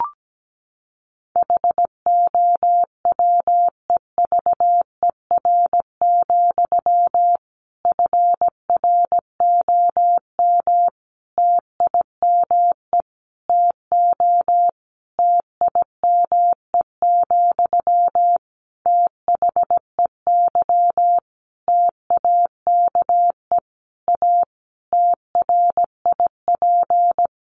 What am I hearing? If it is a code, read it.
Morse code, heard as a keyed tone, about 17 wpm, 700 Hz. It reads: HOWEVER, FROM TIME TO TIME, THEY TAKE A TRIP
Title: Sentences from Top 1300 Words 17wpm